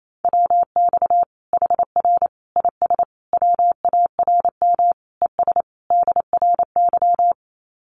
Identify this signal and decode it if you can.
Morse code, heard as a keyed tone, about 28 wpm, 700 Hz. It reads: WX HR IS WARM ES DRY
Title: Common QSO Phrases 28wpm